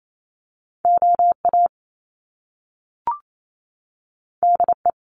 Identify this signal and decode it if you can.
Morse code, heard as a keyed tone, about 28 wpm, 700 Hz. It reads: OA DE
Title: Top Two Letter Combinations in Words 28wpm